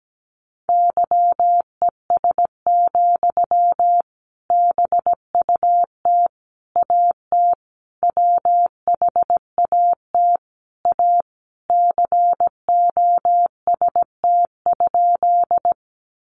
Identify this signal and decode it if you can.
Morse code, heard as a keyed tone, about 17 wpm, 700 Hz. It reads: YES, BUT AT WHAT A COST?
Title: Sentences from Top 800 Words 17wpm